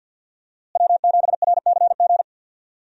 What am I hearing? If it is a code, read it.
Morse code, heard as a keyed tone, about 52 wpm, 700 Hz. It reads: W7RCG
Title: USA Call Signs - 5 Characters - Repeated 2x - 50wpm